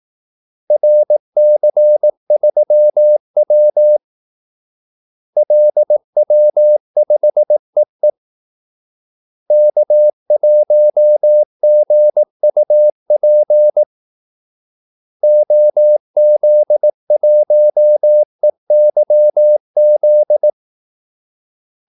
Morse code, heard as a keyed tone, about 18 wpm, 600 Hz. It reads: RC3W LW5EE K1GUP OZ1EYZ
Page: Kaldesignaler 18 wpm